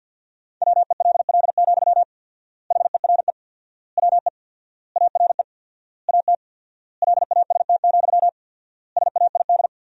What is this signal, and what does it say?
Morse code, heard as a keyed tone, about 50 wpm, 700 Hz. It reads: WELL, HERE WE ARE AT LAST, SAID
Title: 4x Word-Spacing - Sentences from Top 300 Words - 50wpm